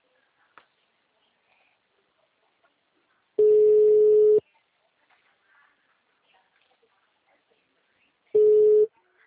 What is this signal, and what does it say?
Ringback tone (Europe, 425 Hz).